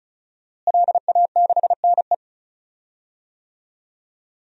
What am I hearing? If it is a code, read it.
Morse code, heard as a keyed tone, about 36 wpm, 700 Hz. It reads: LA6NE
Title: DXCC-100 Call Signs - 5 Characters - Repeated 3x - 35wpm